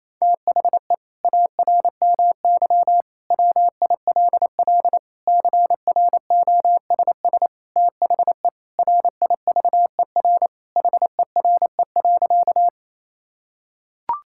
Morse code, heard as a keyed tone, about 28 wpm, 700 Hz. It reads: THE ARMY WILL CROSS THE RIVER HERE.
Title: Sentences from Top 1000 Words - Review Part 5 - 28wpm